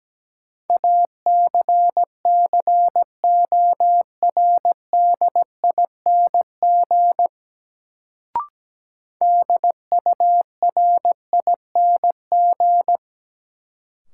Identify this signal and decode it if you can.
Morse code, heard as a keyed tone, about 17 wpm, 700 Hz. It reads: ACCORDING DURING
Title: Sentences from Top 700 Words - New Word Drill - 17wpm